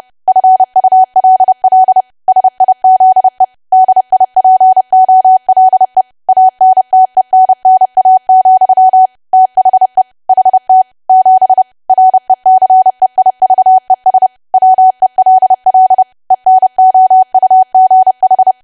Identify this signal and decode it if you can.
Morse code, heard as a keyed tone, about 30 wpm, 750 Hz. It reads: FULL SIZE DIPOLE ANTENNA, THE HT 7 RECEIVES WELL ENOUGH
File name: cwa / W1AW-Level3 / 30WPM